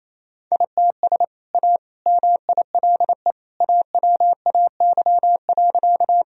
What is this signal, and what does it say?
Morse code, heard as a keyed tone, about 28 wpm, 700 Hz. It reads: ITS A MILE AWAY.
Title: Sentences from Top 1000 Words - Review Part 1 - 28wpm